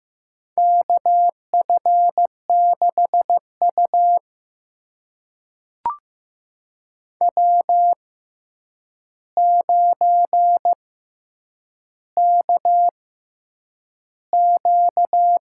Morse code, heard as a keyed tone, about 15 wpm, spaced as for 8 wpm, 700 Hz. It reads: KF6U W9KQ
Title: USA Call Signs - 4 Characters - 6x Character Spacing - 15wpm